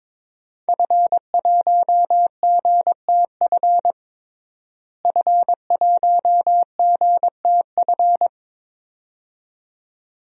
Morse code, heard as a keyed tone, about 22 wpm, 700 Hz. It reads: F1GTF F1GTF
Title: DXCC-100 Call Signs - 5 Characters - Repeated 2x - 22wpm